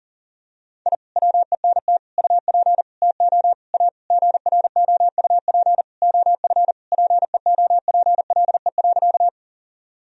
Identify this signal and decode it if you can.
Morse code, heard as a keyed tone, about 37 wpm, 700 Hz. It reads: I WENT UP TO A GROUP OF PEOPLE.
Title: Sentences from Top 1000 Words - Review Part 1 - 40wpm